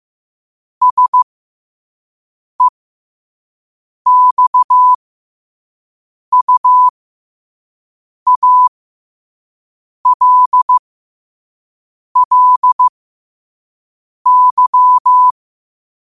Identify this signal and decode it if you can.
Morse code, heard as a keyed tone, about 15 wpm, spaced as for 5 wpm, 1000 Hz. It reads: SEXUALLY
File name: qotdfiles / 5wpm